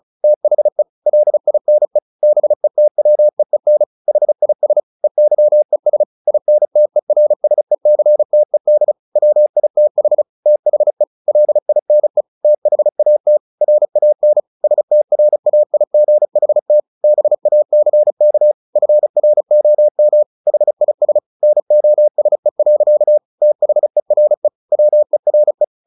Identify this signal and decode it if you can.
Morse code, heard as a keyed tone, about 34 wpm, 600 Hz. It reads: THE LINE BETWEEN HIS EYES INTERSECTED WITH THE LINE THAT RAN STRAIGHT BACK FROM HIS NOSE. THERE WERE
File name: the old man and the sea 2 35wpm